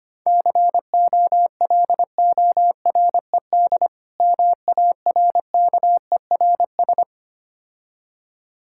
Morse code, heard as a keyed tone, about 25 wpm, 690 Hz. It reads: COLORED MARKERS
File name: practice / 25wpm / Difficult Words with Hints - 25wpm